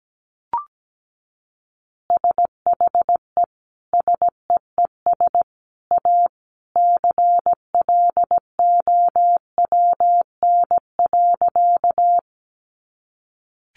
Morse code, heard as a keyed tone, about 17 wpm, 700 Hz. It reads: SHE SEES A CLOWN.
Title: Easy Phrases 17wpm